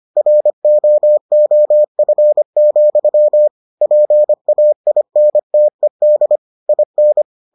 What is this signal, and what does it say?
Morse code, heard as a keyed tone, about 25 wpm, 600 Hz. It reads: ROOF, PAINTED IN